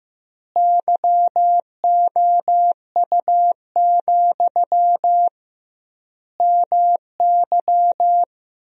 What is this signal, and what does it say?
Morse code, heard as a keyed tone, about 15 wpm, 700 Hz. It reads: YOU, MY
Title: 2x Word-Spacing - Sentences from Top 400 Words - 15wpm